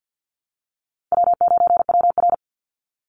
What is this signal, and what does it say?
Morse code, heard as a keyed tone, about 48 wpm, 700 Hz. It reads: W9WR
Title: Easy Call Signs Repeated 3x - 50wpm